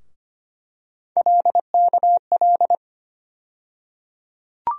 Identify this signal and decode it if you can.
Morse code, heard as a keyed tone, about 25 wpm, 700 Hz. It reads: LKL
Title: USA Call Sign Postfixes - 3 Letters 25wpm